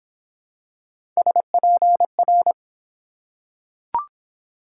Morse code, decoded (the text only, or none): SPR